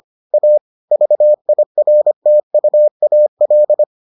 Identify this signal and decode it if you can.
Morse code, heard as a keyed tone, about 25 wpm, 600 Hz. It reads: A VIRTUAL